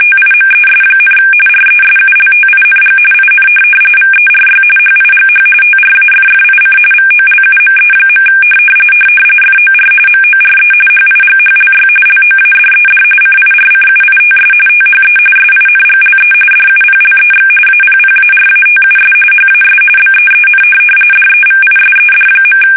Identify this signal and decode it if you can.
RTTY, about 45.45 baud. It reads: TIMES, RESULTING IN VERY HIGH TRAFFIC VOLUME. FOR MORE HIGHLY LOADED CIRCUITS THE SINGLE DUPLEX APPARATUS HAS BEEN DEVISED.
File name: fsk850